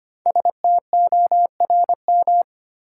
Morse code, heard as a keyed tone, about 25 wpm, 700 Hz. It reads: STORM
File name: 5 Letter Words - First 3 Letters Given - 25wpm